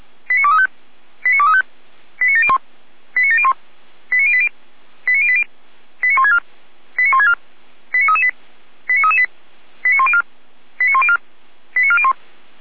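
Five-tone selective call (ZVEI), heard as quick runs of five tones, tones of about 70 ms. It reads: 89235 89235 89701 89701 89080 89080 89164 89164 89338 89338 89104 89104 89401
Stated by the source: ZVEI five-tone selective call received with an SDR.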